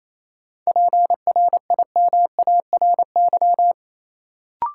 Morse code, heard as a keyed tone, about 28 wpm, 700 Hz. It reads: PRIMARY
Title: Sentences from Top 1000 Words - Review Part 5 - New Word Drill - 28wpm